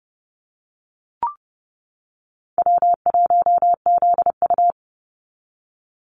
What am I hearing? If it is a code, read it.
Morse code, heard as a keyed tone, about 30 wpm, 700 Hz. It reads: W1ZU
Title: NA SOTA Activator Callsigns 30wpm